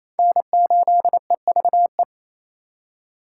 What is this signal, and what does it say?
Morse code, heard as a keyed tone, about 35 wpm, 700 Hz. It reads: N8EVE
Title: Speed Racing - USA 1x3 Call Signs 28wpm